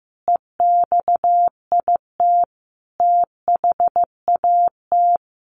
Morse code, heard as a keyed tone, about 15 wpm, 700 Hz. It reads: EXIT THAT
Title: Sentences from Top 1100 Words 15wpm